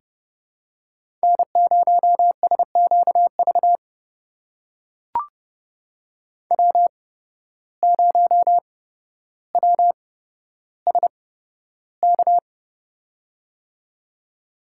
Morse code, heard as a keyed tone, about 30 wpm, spaced as for 13 wpm, 700 Hz. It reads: N0SQV W0WSK
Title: USA Call Signs - 5 Characters - 8x Character Spacing - 30wpm